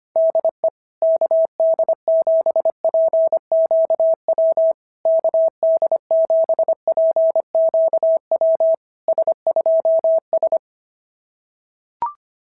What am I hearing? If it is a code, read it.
Morse code, heard as a keyed tone, about 24 wpm, 650 Hz. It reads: DE KD7PQW KD7PQW S2S
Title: SOTA QSO Practice 25wpm